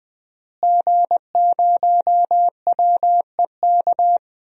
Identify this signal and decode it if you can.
Morse code, heard as a keyed tone, about 21 wpm, 700 Hz. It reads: G0WEK
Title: DXCC-100 Call Signs - 5 Characters - Repeated 2x - 20wpm